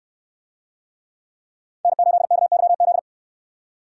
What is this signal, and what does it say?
Morse code, heard as a keyed tone, about 64 wpm, 700 Hz. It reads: N8KYZ